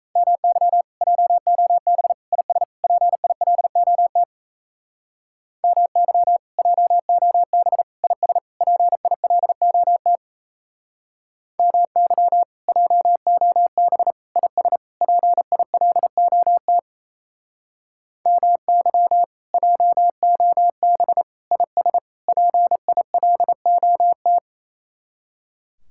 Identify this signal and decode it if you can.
Morse code, heard as a keyed tone, about 35 wpm, 700 Hz. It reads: MY JOB IS PILOT MY JOB IS PILOT MY JOB IS PILOT MY JOB IS PILOT
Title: Speed-Racing Common QSO Phrases - 28wpm